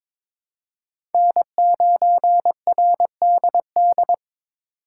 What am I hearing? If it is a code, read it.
Morse code, heard as a keyed tone, about 22 wpm, 700 Hz. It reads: N9RDD
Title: USA Call Signs - 5 Characters - 3x Character Spacing - 22wpm